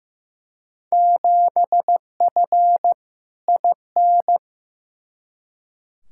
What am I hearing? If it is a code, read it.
Morse code, heard as a keyed tone, about 15 wpm, 700 Hz. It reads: 7F IN
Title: Field Day Exchanges Encore - 15wpm